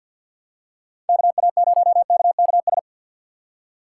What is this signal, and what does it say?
Morse code, heard as a keyed tone, about 52 wpm, 700 Hz. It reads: KA0KKS
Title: USA Call Signs - 6 Characters - 6x Character Spacing - 50wpm